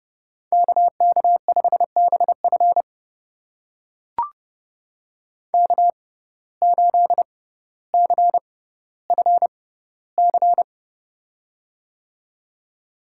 Morse code, heard as a keyed tone, about 30 wpm, spaced as for 15 wpm, 700 Hz. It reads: KK5BF K8CFC
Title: USA Call Signs - 6x Character Spacing - 30wpm